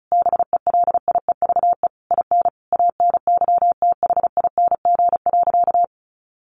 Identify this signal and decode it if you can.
Morse code, heard as a keyed tone, about 34 wpm, 700 Hz. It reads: BELIEVE IN ANYTHING.
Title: Sentences from Top 1500 Words 35wpm